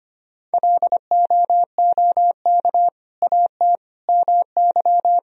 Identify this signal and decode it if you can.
Morse code, heard as a keyed tone, about 25 wpm, 700 Hz. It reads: LOOK AT MY
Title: Sentences from Top 1000 Words - Review Part 3 - 25wpm